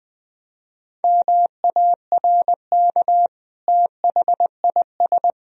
Morse code, heard as a keyed tone, about 21 wpm, 700 Hz. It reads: MARK THIS
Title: Sentences from Top 1000 Words - Review Part 1 - 20wpm